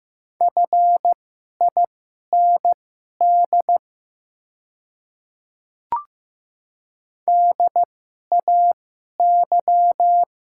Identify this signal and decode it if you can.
Morse code, heard as a keyed tone, about 15 wpm, spaced as for 11 wpm, 700 Hz. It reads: FIND DAY
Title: 2x Character Spacing - Top 100 Words - 15wpm